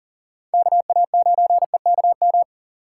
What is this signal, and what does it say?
Morse code, heard as a keyed tone, about 38 wpm, 700 Hz. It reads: KA9EKM
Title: USA Call Signs - 6 Characters - 4x Character Spacing - 40wpm